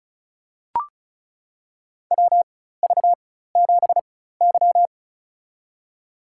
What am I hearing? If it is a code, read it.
Morse code, heard as a keyed tone, about 34 wpm, spaced as for 16 wpm, 690 Hz. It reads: WV7Y
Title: USA Call Signs - 4x Character Spacing - 35wpm